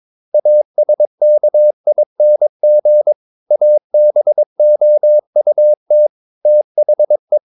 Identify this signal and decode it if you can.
Morse code, heard as a keyed tone, about 22 wpm, 600 Hz. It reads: ASKING ABOUT THE
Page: Romanen Never af Ken Follet på engelsk.